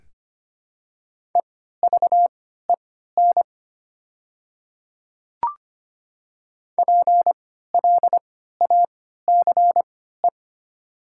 Morse code, heard as a keyed tone, about 25 wpm, spaced as for 14 wpm, 690 Hz. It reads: EVEN PLACE
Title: 3x Character Spacing - Top 300 Words - 25wpm